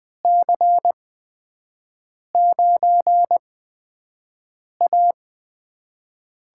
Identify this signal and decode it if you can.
Morse code, heard as a keyed tone, about 20 wpm, spaced as for 5 wpm, 700 Hz. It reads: C9A
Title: USA Call Signs - 8x Character Spacing - 20wpm